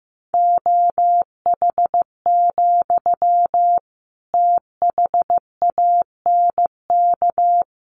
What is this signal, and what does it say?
Morse code, heard as a keyed tone, about 15 wpm, 700 Hz. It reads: OH, THANK
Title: Sentences from Top 600 Words 15wpm